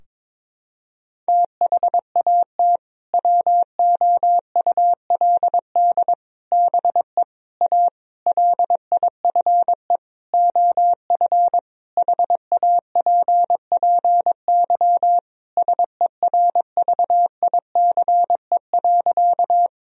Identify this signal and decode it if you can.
Morse code, heard as a keyed tone, about 22 wpm, 700 Hz. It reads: THAT WOULD BE A LIFE OF HAPPY SERVICE.
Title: Sentences from Top 1000 Words - Review Part 4 - 22wpm